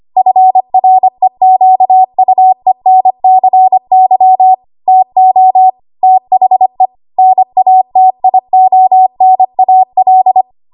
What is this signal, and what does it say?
Morse code, heard as a keyed tone, about 25 wpm, 750 Hz. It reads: FREQUENCY TO THE NATIONAL
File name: Morse / 25WPM